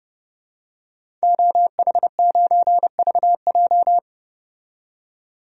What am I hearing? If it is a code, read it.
Morse code, heard as a keyed tone, about 30 wpm, 700 Hz. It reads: OH9VJ
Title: DXCC-100 Call Signs - 5 Characters - 30wpm